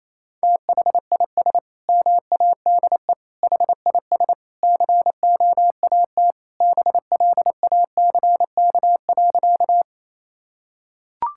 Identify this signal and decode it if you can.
Morse code, heard as a keyed tone, about 28 wpm, 700 Hz. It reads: THIS MADE HIS COAT BLACK.
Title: Sentences from Top 1600 Words 28wpm